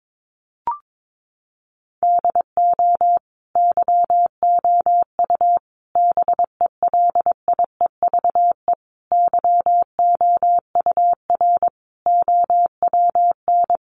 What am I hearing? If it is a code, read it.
Morse code, heard as a keyed tone, about 22 wpm, 700 Hz. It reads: DO YOU BELIEVE YOUR OWN
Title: Sentences from Top 1000 Words 22wpm